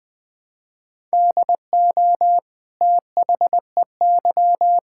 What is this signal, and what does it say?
Morse code, heard as a keyed tone, about 21 wpm, 700 Hz. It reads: DO THEY
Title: Sentences from Top 1000 Words - Review Part 6 - 20wpm